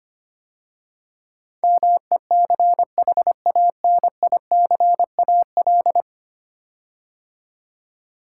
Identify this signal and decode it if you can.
Morse code, heard as a keyed tone, about 25 wpm, 700 Hz. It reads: MECHANICAL
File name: 10 Letter Words - First 3 Letters Given - 25wpm